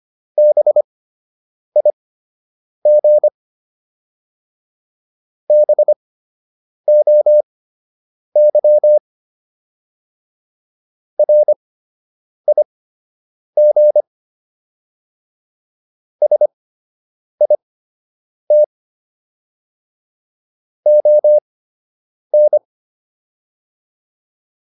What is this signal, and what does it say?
Morse code, heard as a keyed tone, about 25 wpm, spaced as for 8 wpm, 600 Hz. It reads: BIG BOY RIG SIT ON